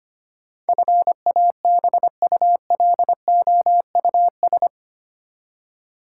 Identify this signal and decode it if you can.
Morse code, heard as a keyed tone, about 25 wpm, 700 Hz. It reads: FABULOUS
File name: 8 Letter Words - First 3 Letters Given - 25wpm